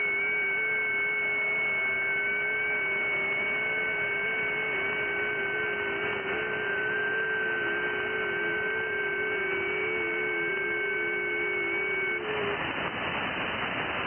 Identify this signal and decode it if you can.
FT8.
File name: IC-7100